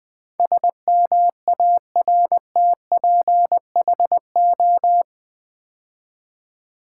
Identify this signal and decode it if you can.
Morse code, heard as a keyed tone, about 20 wpm, 700 Hz. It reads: SMARTPHO
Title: Build A Word - 4 Letters At A Time - 10-Letter Words - 20wpm